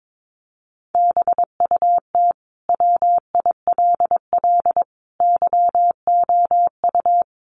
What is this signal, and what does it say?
Morse code, heard as a keyed tone, about 22 wpm, 700 Hz. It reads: BUT WILL YOU
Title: Sentences from Top 1400 Words 22wpm